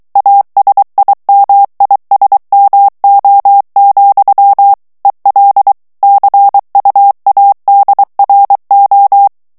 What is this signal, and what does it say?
Morse code, heard as a keyed tone, about 23 wpm, 800 Hz. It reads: ASIMISMO, EL CUADRO
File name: morse mp3 / ESPAGNOL TEXTE EN CLAIR 2 VITESSE 22 MOTS MINUTES